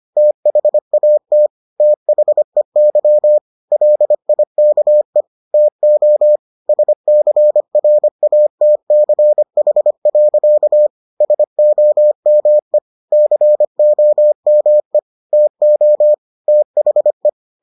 Morse code, heard as a keyed tone, about 25 wpm, 600 Hz. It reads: THAT THEY LIKE TO SCRATCH. SOME COME TO THE